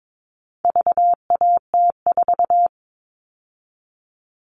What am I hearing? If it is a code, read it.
Morse code, heard as a keyed tone, about 22 wpm, 700 Hz. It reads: VAT4
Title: All Cut Numbers - 4-Digit 22wpm